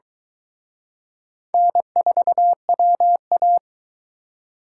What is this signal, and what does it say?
Morse code, heard as a keyed tone, about 23 wpm, 700 Hz. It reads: N4WA